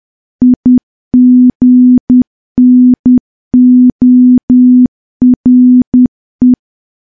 Morse code, heard as a keyed tone, about 10 wpm, 250 Hz. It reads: IGNORE
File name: morse / morse-training-10WPM